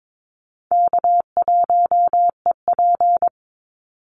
Morse code, heard as a keyed tone, about 22 wpm, 700 Hz. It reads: K1EP
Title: Call Signs Repeated 3x - 22wpm